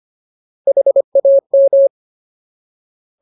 Morse code, heard as a keyed tone, about 25 wpm, 550 Hz. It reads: HAM
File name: Ultimate Word List / 25WPM Ultimate